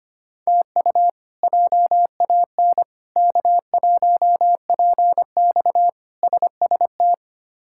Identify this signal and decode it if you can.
Morse code, heard as a keyed tone, about 25 wpm, 700 Hz. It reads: TU JAN K1PX SST
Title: K1USN SST QSO Practice - 25wpm